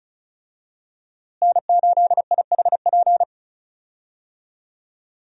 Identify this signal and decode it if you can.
Morse code, heard as a keyed tone, about 36 wpm, 700 Hz. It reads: N8IHP
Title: USA Call Signs - 5 Characters - Repeated 3x - 35wpm